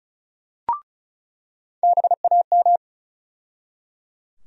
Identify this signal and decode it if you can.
Morse code, heard as a keyed tone, about 36 wpm, 700 Hz. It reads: BAM
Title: USA Call Sign Postfixes - 3 Letters 35wpm